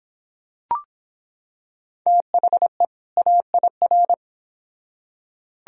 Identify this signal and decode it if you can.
Morse code, heard as a keyed tone, about 26 wpm, 700 Hz. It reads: THE AIR